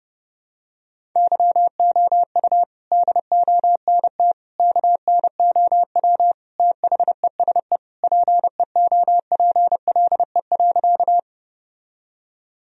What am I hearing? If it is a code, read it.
Morse code, heard as a keyed tone, about 32 wpm, 700 Hz. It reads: YOU DONT KNOW THESE PEOPLE.
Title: Sentences from Top 1000 Words - Review Part 1 - 30wpm